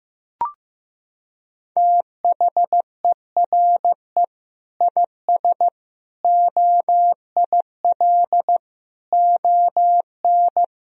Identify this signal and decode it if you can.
Morse code, heard as a keyed tone, about 15 wpm, 700 Hz. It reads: THERE IS OIL ON
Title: Sentences from Top 1000 Words - Review Part 4 - 15wpm